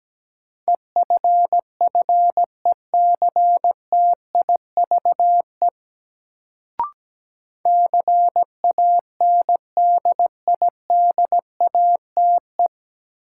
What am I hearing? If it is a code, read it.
Morse code, heard as a keyed tone, about 17 wpm, 700 Hz. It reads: EFFECTIVE CANDIDATE
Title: Sentences from Top 1600 Words - New Word Drill Alt - 17wpm